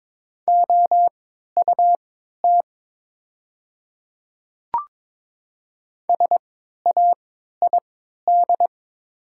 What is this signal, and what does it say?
Morse code, heard as a keyed tone, about 22 wpm, spaced as for 13 wpm, 700 Hz. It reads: OUT SAID
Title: 3x Character Spacing - Top 100 Words - 22wpm